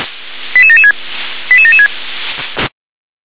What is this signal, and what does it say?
Five-tone selective call (ZVEI), heard as quick runs of five tones, tones of about 70 ms. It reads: 88776 88776